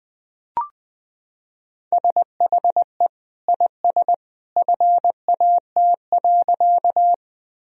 Morse code, heard as a keyed tone, about 20 wpm, 700 Hz. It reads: SHE IS FAT.
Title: Sentences from Top 1000 Words - Review Part 4 - 20wpm